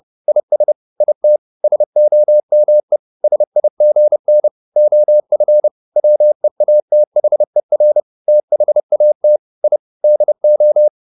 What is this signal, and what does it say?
Morse code, heard as a keyed tone, about 30 wpm, 600 Hz. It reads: IS IT SOME SIGN OF WEATHER THAT I DO
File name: the old man and the sea / the old man and the sea 1 30wpm